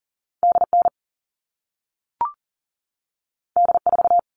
Morse code, heard as a keyed tone, about 40 wpm, 700 Hz. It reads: DN D4
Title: All Cut Numbers - 2-Digit 40wpm